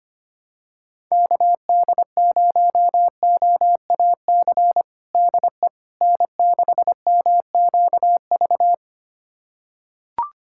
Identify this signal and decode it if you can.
Morse code, heard as a keyed tone, about 25 wpm, 700 Hz. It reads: KD0OAC DE N6MQV
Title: USA Callsign DE Callsign 25wpm